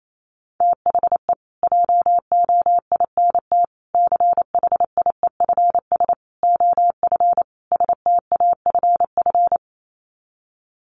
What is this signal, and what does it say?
Morse code, heard as a keyed tone, about 28 wpm, 700 Hz. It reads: THE JOINT CHIEFS OF STAFF
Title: Sets of 5 Words Encore2 28wpm